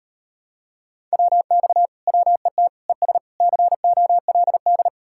Morse code, heard as a keyed tone, about 38 wpm, 700 Hz. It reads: WX WET ES COLD